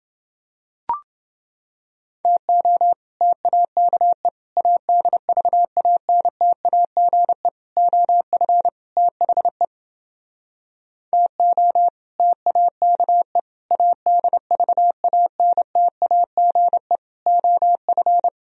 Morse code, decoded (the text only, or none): TO TAKE ADVANTAGE OF THE TO TAKE ADVANTAGE OF